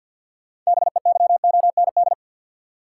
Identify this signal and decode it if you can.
Morse code, heard as a keyed tone, about 51 wpm, 700 Hz. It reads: BEYOND
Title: Sentences from Top 1000 Words - Review Part 6 - New Word Drill - 50wpm